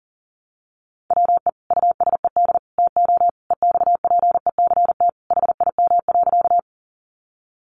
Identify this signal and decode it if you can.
Morse code, heard as a keyed tone, about 38 wpm, 700 Hz. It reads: WE USED TO EXPECT HIM.
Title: Sentences from Top 1100 Words 40wpm